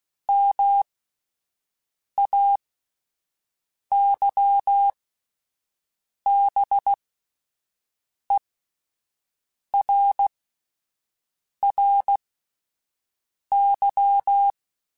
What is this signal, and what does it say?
Morse code, heard as a keyed tone, about 16 wpm, spaced as for 5 wpm, 800 Hz. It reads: MAYBERRY